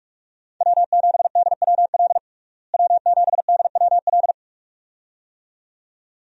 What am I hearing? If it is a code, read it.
Morse code, heard as a keyed tone, about 45 wpm, 700 Hz. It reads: W7DWL W7DWL
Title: USA Call Signs - 5 Characters - Repeated 2x - 45wpm